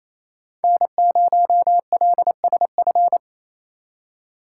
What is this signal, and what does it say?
Morse code, heard as a keyed tone, about 36 wpm, 700 Hz. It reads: N0LSF